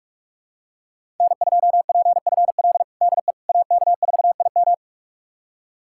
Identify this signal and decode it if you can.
Morse code, heard as a keyed tone, about 46 wpm, 700 Hz. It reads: N2PFL DE AK4IM
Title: USA Callsign DE Callsign - 5 Characters - 45wpm